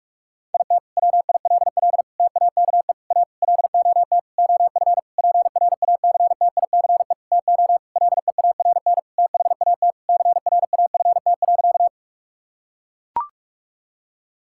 Morse code, heard as a keyed tone, about 42 wpm, 700 Hz. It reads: IT WILL TAKE A LOT OF PRACTICE TO LEARN THAT CRAFT.
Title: Sentences from Top 1000 Words - Review Part 6 - 45wpm